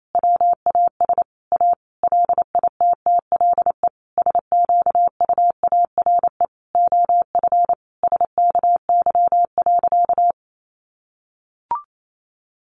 Morse code, heard as a keyed tone, about 28 wpm, 700 Hz. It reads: WAS A LITTLE SQUARE OF SKY.
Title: Sentences from Top 1200 Words 28wpm